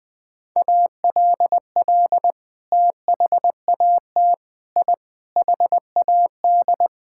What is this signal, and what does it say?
Morse code, heard as a keyed tone, about 20 wpm, 700 Hz. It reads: ALL THAT I HAD
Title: Sentences from Top 1000 Words - Review Part 6 - 20wpm